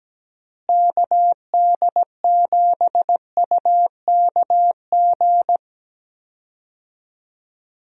Morse code, heard as a keyed tone, about 17 wpm, 700 Hz. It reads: KD7UKG
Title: USA Call Signs - 6 Characters - Repeated 2x - 17wpm